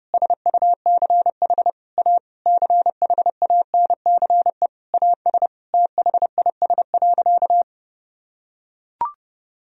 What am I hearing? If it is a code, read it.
Morse code, heard as a keyed tone, about 30 wpm, 700 Hz. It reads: SUCH A CHANCE AS THIS.
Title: Sentences from Top 1000 Words - Review Part 1 - 30wpm